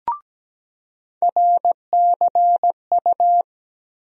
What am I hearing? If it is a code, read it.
Morse code, heard as a keyed tone, about 17 wpm, 700 Hz. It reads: RCU
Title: USA Call Sign Postfixes - 3 Letters 17wpm